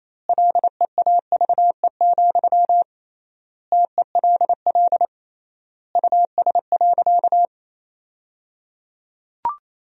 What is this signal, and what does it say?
Morse code, heard as a keyed tone, about 28 wpm, 700 Hz. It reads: LEAVE, TELL US.
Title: 3x Word-Spacing - Sentences from Top 400 Words - 28wpm